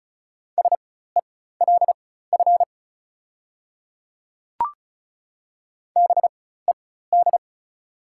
Morse code, heard as a keyed tone, about 36 wpm, spaced as for 16 wpm, 690 Hz. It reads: SELF BED
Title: 4x Character Spacing - Top 500 Words - 35wpm